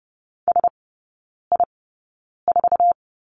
Morse code, heard as a keyed tone, about 30 wpm, spaced as for 9 wpm, 700 Hz. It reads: SI4
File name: Lesson 17e - Letter u - Three Characters 30wpm